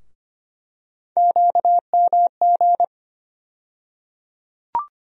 Morse code, heard as a keyed tone, about 25 wpm, 700 Hz. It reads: QMG
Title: USA Call Sign Postfixes - 3 Letters 25wpm